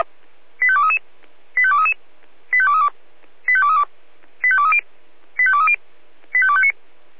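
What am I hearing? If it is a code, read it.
Five-tone selective call (ZVEI), heard as quick runs of five tones, tones of about 70 ms. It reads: 85233 85233 85232 85232 85230 85230 85259
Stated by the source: ZVEI five-tone call 85885 received by SDR, channel 496.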